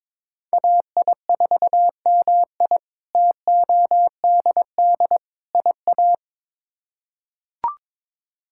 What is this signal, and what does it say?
Morse code, heard as a keyed tone, about 22 wpm, 700 Hz. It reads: AI4MI TODD IA
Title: K1USN SST QSO Practice - 22wpm